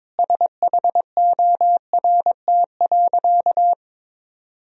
Morse code, heard as a keyed tone, about 22 wpm, 700 Hz. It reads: SHORT.
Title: Sentences from Top 1000 Words - Review Part 2 - 22wpm